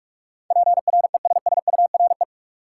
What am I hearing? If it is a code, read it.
Morse code, heard as a keyed tone, about 46 wpm, 700 Hz. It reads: PRESSURE
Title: Sentences from Top 1000 Words - Review Part 6 - New Word Drill - 45wpm